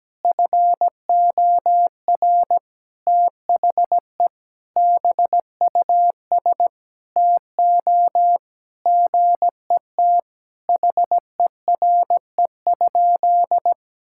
Morse code, heard as a keyed tone, about 17 wpm, 700 Hz. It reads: FOR THE BUS TO GET HERE?
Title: Sentences from Top 1000 Words - Review Part 6 - 17wpm